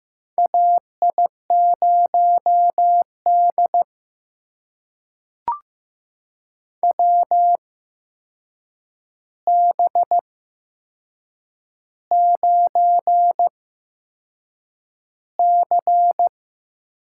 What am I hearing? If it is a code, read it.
Morse code, heard as a keyed tone, about 15 wpm, spaced as for 6 wpm, 700 Hz. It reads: AI0D WB9C
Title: USA Call Signs - 4 Characters - 8x Character Spacing - 15wpm